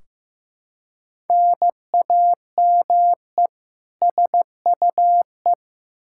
Morse code, heard as a keyed tone, about 15 wpm, 700 Hz. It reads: NAME SUE
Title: Practice QSO - Exchange 1 - Beginning - 15wpm